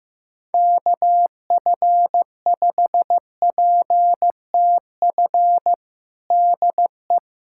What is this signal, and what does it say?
Morse code, heard as a keyed tone, about 15 wpm, 700 Hz. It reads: KF5PTF DE
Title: USA Callsign DE Callsign - 6 Characters - 15wpm